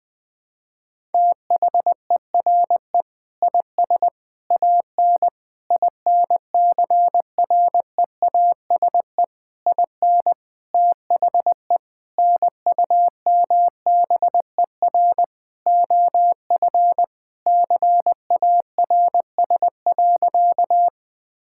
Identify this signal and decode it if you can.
Morse code, heard as a keyed tone, about 20 wpm, 700 Hz. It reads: THERE IS AN INCREASE IN THE NUMBER OF CARS.
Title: Sentences with 5-Word Phrases - 20wpm